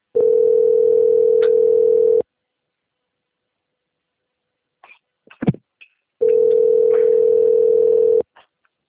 Ringback tone (North America), 440 + 480 Hz.